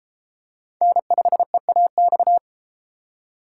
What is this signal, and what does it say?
Morse code, heard as a keyed tone, about 30 wpm, 700 Hz. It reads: N5EAX